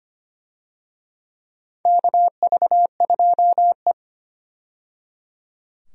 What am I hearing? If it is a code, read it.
Morse code, heard as a keyed tone, about 25 wpm, 700 Hz. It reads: KV2E
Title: USA Call Signs - 4 Characters - 25wpm